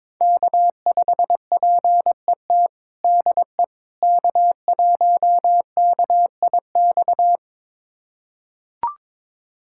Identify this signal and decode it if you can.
Morse code, heard as a keyed tone, about 22 wpm, 700 Hz. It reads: K5PET DE K1KIX
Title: USA Callsign DE Callsign - 5 Characters - 22wpm